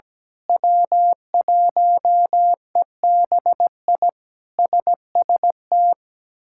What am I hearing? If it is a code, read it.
Morse code, heard as a keyed tone, about 17 wpm, 700 Hz. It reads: W1EBI SST
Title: K1USN SST QSO Practice - 17wpm